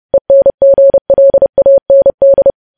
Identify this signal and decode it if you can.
Morse code, heard as a keyed tone, about 30 wpm, 550 Hz. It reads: ENGLAND
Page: news headlines in morse code at 30 wpm